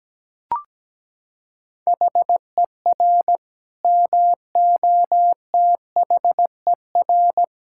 Morse code, heard as a keyed tone, about 17 wpm, 700 Hz. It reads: HER MOTHER
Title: Sentences from Top 1000 Words - Review Part 5 - 17wpm